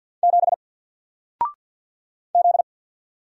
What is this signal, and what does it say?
Morse code, heard as a keyed tone, about 48 wpm, 700 Hz. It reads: 7 Z
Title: Single Character (Letter:Number:Punctuation:Prosign) Alt 50wpm